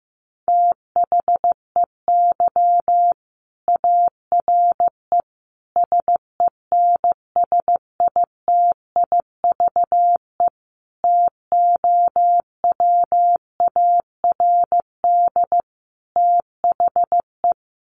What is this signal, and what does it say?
Morse code, heard as a keyed tone, about 15 wpm, 700 Hz. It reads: THEY ARE SENSITIVE TOWARD THE
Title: Sentences from Top 1400 Words 15wpm